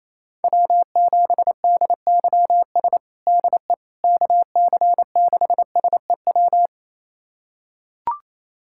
Morse code, heard as a keyed tone, about 28 wpm, 700 Hz. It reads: W7DYS DE KC6SEW
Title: USA Callsign DE Callsign 28wpm